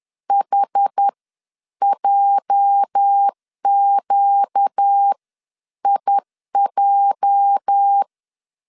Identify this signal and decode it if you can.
Morse code, heard as a keyed tone, about 10 wpm, spaced as for 9 wpm, 800 Hz. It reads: HJQIJ
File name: LEZIONI / 8 YJ45 / 8 YJ 10wpm 2651